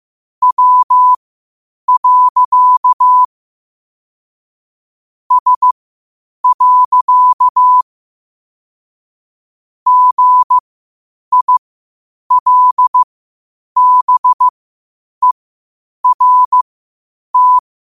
Morse code, heard as a keyed tone, about 15 wpm, spaced as for 8 wpm, 1000 Hz. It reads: W. S. GILBERT
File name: qotdfiles / 8wpm